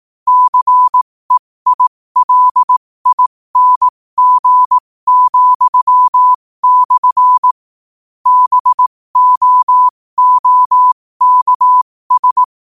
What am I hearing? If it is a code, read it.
Morse code, heard as a keyed tone, about 18 wpm, spaced as for 15 wpm, 1000 Hz. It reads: CEILING,/ BOOKS
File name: qotdfiles / 15wpm